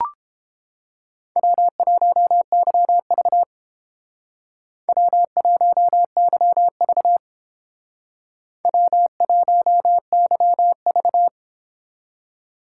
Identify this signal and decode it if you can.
Morse code, heard as a keyed tone, about 28 wpm, 700 Hz. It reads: W1YV W1YV W1YV